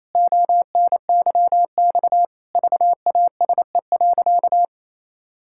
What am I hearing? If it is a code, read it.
Morse code, heard as a keyed tone, about 28 wpm, 700 Hz. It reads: ONYX VASE.
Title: Pangram Challenge v2 - 28wpm